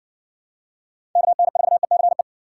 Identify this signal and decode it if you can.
Morse code, heard as a keyed tone, about 54 wpm, 700 Hz. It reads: KN4ECE